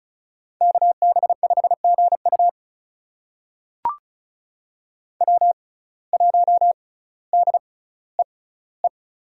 Morse code, heard as a keyed tone, about 36 wpm, spaced as for 18 wpm, 700 Hz. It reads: KB5GU W1DEE
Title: USA Call Signs - 5 Characters - 6x Character Spacing - 35wpm